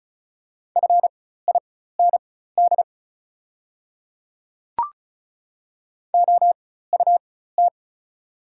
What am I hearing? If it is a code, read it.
Morse code, heard as a keyed tone, about 36 wpm, spaced as for 16 wpm, 700 Hz. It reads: FIND OUT
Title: 4x Character Spacing - Top 100 Words - 35wpm